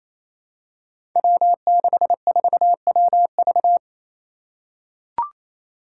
Morse code, heard as a keyed tone, about 28 wpm, 700 Hz. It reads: W64WV
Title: Five Challenging Letter-Number Combinations 28wpm